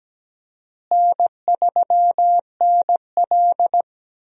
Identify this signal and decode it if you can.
Morse code, heard as a keyed tone, about 17 wpm, 700 Hz. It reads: N3NL
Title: USA Call Signs - 4 Characters - Repeated 2x - 17wpm